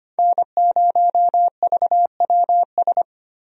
Morse code, heard as a keyed tone, about 25 wpm, 700 Hz. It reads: N0VWS
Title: USA Call Signs - 5 Characters - Repeated 3x - 25wpm